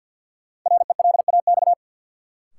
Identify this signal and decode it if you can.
Morse code, heard as a keyed tone, about 52 wpm, 700 Hz. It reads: RELAX
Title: Sentences from Top 1400 Words - New Word Drill - 50wpm